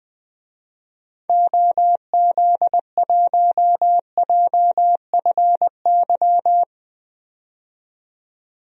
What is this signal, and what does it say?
Morse code, heard as a keyed tone, about 20 wpm, 700 Hz. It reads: OZ1JFY
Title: DXCC-100 Call Signs - 6 Characters - 20wpm